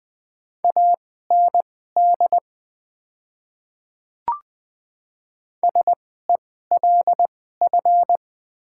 Morse code, heard as a keyed tone, about 20 wpm, spaced as for 14 wpm, 700 Hz. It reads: AND SELF
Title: 2x Character Spacing - Top 300 Words - 20wpm